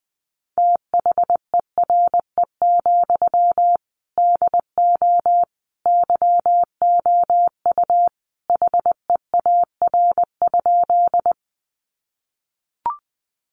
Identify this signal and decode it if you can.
Morse code, heard as a keyed tone, about 20 wpm, 700 Hz. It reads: THERE, DO YOU HEAR?
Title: Sentences from Top 300 Words 20wpm